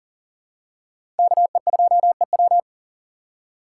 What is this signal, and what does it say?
Morse code, heard as a keyed tone, about 40 wpm, 700 Hz. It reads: KE2EW
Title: USA Call Signs - 5 Characters - 4x Character Spacing - 40wpm